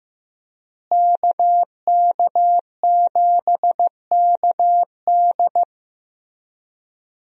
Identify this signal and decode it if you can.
Morse code, heard as a keyed tone, about 15 wpm, 700 Hz. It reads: KK7KD
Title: USA Call Signs - 5 Characters - Repeated 3x - 15wpm